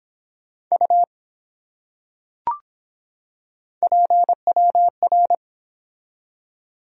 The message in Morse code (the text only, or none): U PWR